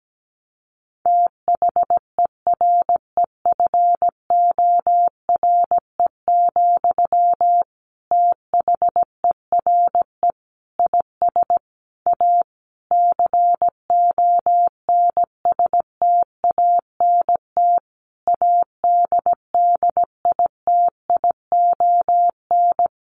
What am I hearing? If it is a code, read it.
Morse code, heard as a keyed tone, about 17 wpm, 700 Hz. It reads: THEREFORE, THERE IS A CONSTANT ADDITION
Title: Sentences from Top 1500 Words 17wpm